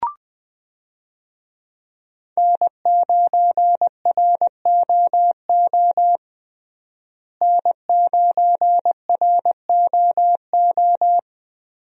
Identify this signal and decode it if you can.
Morse code, heard as a keyed tone, about 20 wpm, 700 Hz. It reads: N9ROO N9ROO
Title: USA Call Signs - 5 Characters - Repeated 3x - 20wpm